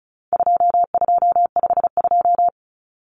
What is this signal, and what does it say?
Morse code, heard as a keyed tone, about 34 wpm, 700 Hz. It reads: 2252
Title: Common Cut Numbers - 4-Digit 35wpm